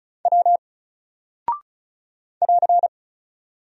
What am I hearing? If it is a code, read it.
Morse code, heard as a keyed tone, about 36 wpm, 700 Hz. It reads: W <AR>
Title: Single Character (Letter:Number:Punctuation:Prosign) 35wpm